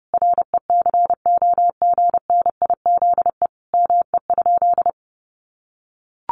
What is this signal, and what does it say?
Morse code, heard as a keyed tone, about 30 wpm, 700 Hz. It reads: RECOGNIZE ME?
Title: Sentences from Top 1400 Words 30wpm